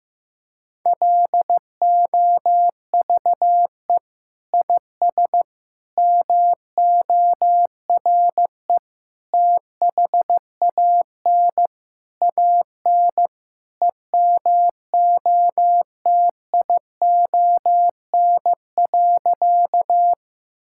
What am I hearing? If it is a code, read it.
Morse code, heard as a keyed tone, about 15 wpm, 700 Hz. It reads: LOVE IS MORE THAN AN EMOTION.
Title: Sentences from Top 1600 Words 15wpm